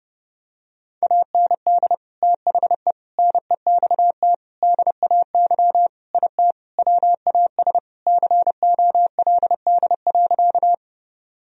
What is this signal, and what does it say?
Morse code, heard as a keyed tone, about 30 wpm, 700 Hz. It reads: AND THE NEXT DAY IT WAS COLD.
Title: Sentences from Top 1000 Words - Review Part 1 - 30wpm